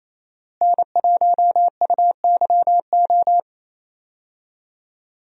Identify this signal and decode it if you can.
Morse code, heard as a keyed tone, about 28 wpm, 700 Hz. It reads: N1UYO
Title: USA Call Signs - 5 Characters - Repeated 3x - 28wpm